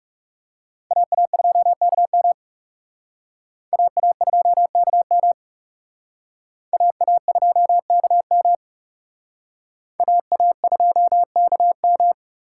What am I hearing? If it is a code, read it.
Morse code, heard as a keyed tone, about 37 wpm, 700 Hz. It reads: AA2KM AA2KM AA2KM AA2KM
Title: Speed Racing - USA Call Signs 30wpm